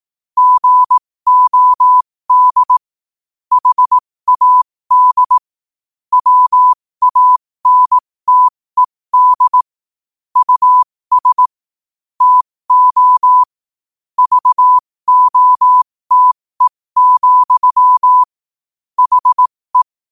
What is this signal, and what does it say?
Morse code, heard as a keyed tone, about 18 wpm, spaced as for 15 wpm, 1000 Hz. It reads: GOD HAD WANTED US TO VOTE, HE
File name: qotdfiles / 15wpm